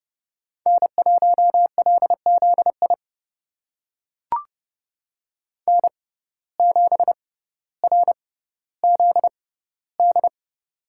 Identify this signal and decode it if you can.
Morse code, heard as a keyed tone, about 30 wpm, spaced as for 15 wpm, 690 Hz. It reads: N1LZI N7RZD
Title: USA Call Signs - 5 Characters - 6x Character Spacing - 30wpm